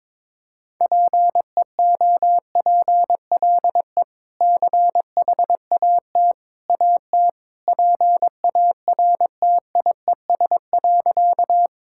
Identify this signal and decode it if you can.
Morse code, heard as a keyed tone, about 21 wpm, 700 Hz. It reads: PEOPLE CHAT AT PARTIES.
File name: practice / 22wpm / Easy Sentences with Hints - 22wpm